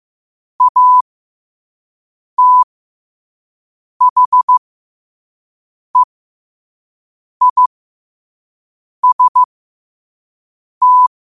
Morse code, heard as a keyed tone, about 15 wpm, spaced as for 5 wpm, 1000 Hz. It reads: ATHEIST
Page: Quotes for Fri, 15 Aug 2025 in Morse Code at 5 words per minute.